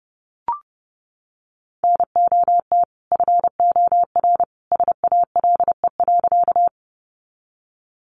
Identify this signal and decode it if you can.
Morse code, heard as a keyed tone, about 30 wpm, 700 Hz. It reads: NOT FOR SALE.
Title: Sentences from Top 1400 Words 30wpm